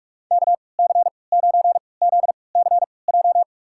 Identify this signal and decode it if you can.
Morse code, heard as a keyed tone, about 45 wpm, spaced as for 25 wpm, 700 Hz. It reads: KC9ZCJ
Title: USA Call Signs - 3x Character Spacing - 45wpm